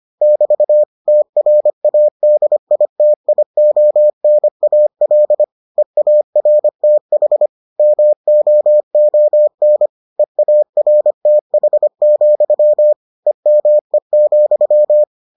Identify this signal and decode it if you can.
Morse code, heard as a keyed tone, about 25 wpm, 600 Hz. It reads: <BT> TRADITIONAL EARTH MOON EARTH, EME,